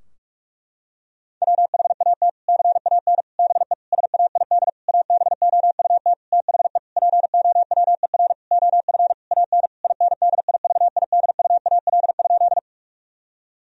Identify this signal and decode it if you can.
Morse code, heard as a keyed tone, about 42 wpm, 700 Hz. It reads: WHAT CAN BE SAID ABOUT THE POWER OF AN INDIVIDUAL?
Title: Sentences from Top 1000 Words - Review Part 4 - 45wpm